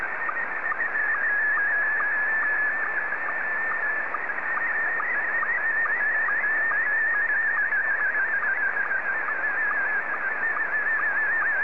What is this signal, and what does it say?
SSTV (slow-scan television), Scottie 1.
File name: sstv / S1-ZL2TJM